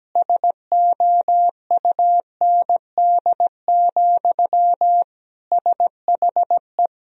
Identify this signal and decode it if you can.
Morse code, heard as a keyed tone, about 17 wpm, 700 Hz. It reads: SOUND, SHE
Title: Sentences from Top 1000 Words - Review Part 1 - 17wpm